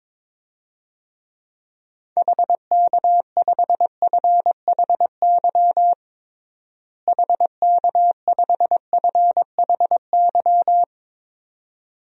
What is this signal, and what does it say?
Morse code, heard as a keyed tone, about 22 wpm, 700 Hz. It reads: HK5FHY HK5FHY
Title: DXCC-100 Call Signs - 6 Characters - Repeated 2x - 22wpm